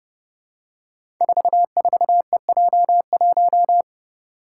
Morse code, heard as a keyed tone, about 28 wpm, 700 Hz. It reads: 44EJ1
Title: Code Groups - 5 Letter:Numbers - 30wpm